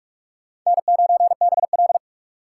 Morse code, heard as a keyed tone, about 59 wpm, spaced as for 52 wpm, 700 Hz. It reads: N9BL
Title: Speed Racing - USA 1x2 Call Signs 45wpm